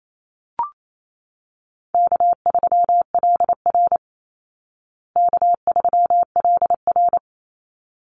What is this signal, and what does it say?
Morse code, heard as a keyed tone, about 28 wpm, 700 Hz. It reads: K3LR K3LR
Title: Call Signs Repeated 3x - 28wpm